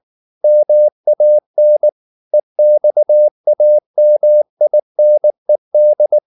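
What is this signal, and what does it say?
Morse code, heard as a keyed tone, about 19 wpm, 600 Hz. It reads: MAN EXAMINED
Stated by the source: Morse code at 19 wpm from CW med Gnister, reading Ken Follett's novel Never in English.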